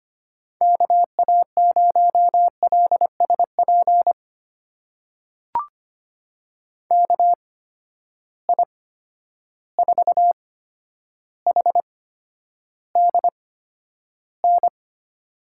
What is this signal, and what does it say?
Morse code, heard as a keyed tone, about 25 wpm, spaced as for 11 wpm, 700 Hz. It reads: KA0LSP KI4HDN
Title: USA Call Signs - 6 Characters - 8x Character Spacing - 25wpm